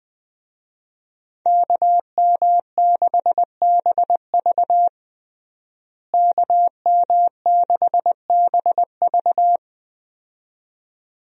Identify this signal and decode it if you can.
Morse code, heard as a keyed tone, about 20 wpm, 700 Hz. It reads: KM6BV KM6BV
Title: USA Call Signs - 5 Characters - Repeated 2x - 20wpm